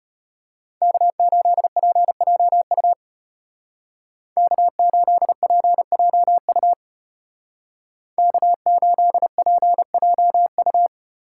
Morse code, heard as a keyed tone, about 30 wpm, 700 Hz. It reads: K8PJU K8PJU K8PJU
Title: Speed Racing - USA 1x3 Call Signs 25wpm